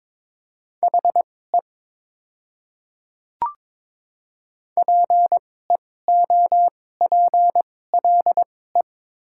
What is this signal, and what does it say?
Morse code, heard as a keyed tone, about 22 wpm, spaced as for 16 wpm, 700 Hz. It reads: HE PEOPLE
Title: 2x Character Spacing - Top 100 Words - 22wpm